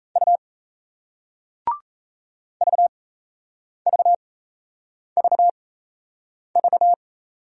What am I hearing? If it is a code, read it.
Morse code, heard as a keyed tone, about 38 wpm, spaced as for 8 wpm, 700 Hz. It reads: U VVVV